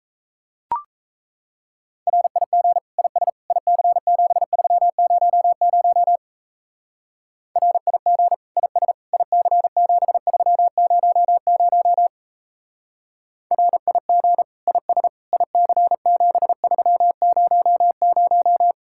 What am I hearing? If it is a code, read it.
Morse code, heard as a keyed tone, about 35 wpm, 700 Hz. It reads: RIG IS IC7300 RIG IS IC7300 RIG IS IC7300